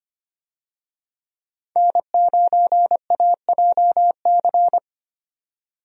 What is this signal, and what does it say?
Morse code, heard as a keyed tone, about 25 wpm, 700 Hz. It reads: N9AJC
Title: USA Call Signs - 5 Characters - 25wpm